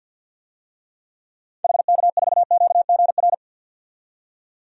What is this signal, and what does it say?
Morse code, heard as a keyed tone, about 47 wpm, 700 Hz. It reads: HK4QGR
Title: DXCC-100 Call Signs - 6 Characters - 50wpm